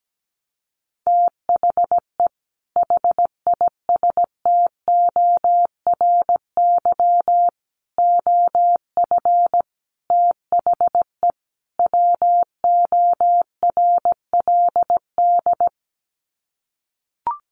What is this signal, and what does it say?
Morse code, heard as a keyed tone, about 17 wpm, 700 Hz. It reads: THE HISTORY OF THE WORLD
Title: Sets of 5 Words Encore2 17wpm